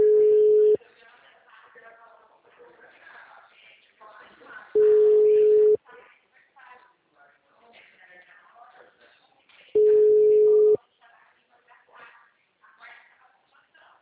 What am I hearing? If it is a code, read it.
Ringback tone (Europe, 425 Hz), with speech.
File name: out- -2016- - - .10